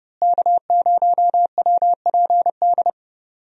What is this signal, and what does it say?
Morse code, heard as a keyed tone, about 30 wpm, 700 Hz. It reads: K0WPD
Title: USA Call Signs - 5 Characters - Repeated 2x - 30wpm